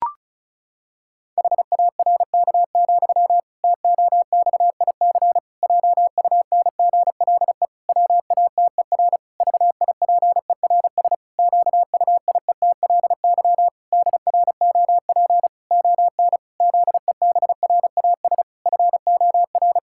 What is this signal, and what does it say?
Morse code, heard as a keyed tone, about 34 wpm, 700 Hz. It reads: HARK, TOXIC JUNGLE WATER VIPERS QUIETLY DROP ON ZEBRAS FOR
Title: Pangram Challenge v2 - 35wpm